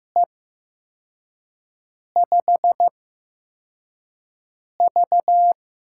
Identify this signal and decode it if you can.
Morse code, heard as a keyed tone, about 15 wpm, spaced as for 4 wpm, 700 Hz. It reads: E5V
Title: USA Call Signs - 6 Characters - 8x Character Spacing - 15wpm